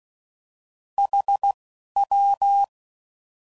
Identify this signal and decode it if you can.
Morse code, heard as a keyed tone, about 16 wpm, 800 Hz. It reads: H W